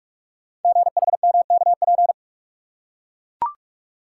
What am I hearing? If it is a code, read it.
Morse code, heard as a keyed tone, about 44 wpm, 700 Hz. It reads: GHMKP
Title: Code Groups - 5 Letters - 45wpm